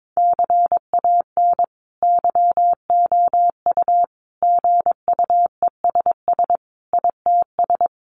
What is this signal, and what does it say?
Morse code, heard as a keyed tone, about 22 wpm, 700 Hz. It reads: CAN YOU GUESS ITS
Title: Sentences from Top 1500 Words 22wpm